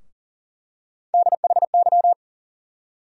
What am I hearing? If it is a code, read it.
Morse code, heard as a keyed tone, about 38 wpm, 700 Hz. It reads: DHY
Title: USA Call Sign Postfixes - 3 Letters 40wpm